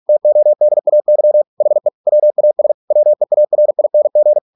Morse code, heard as a keyed tone, about 46 wpm, 600 Hz. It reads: TODAY HE WAS WEARING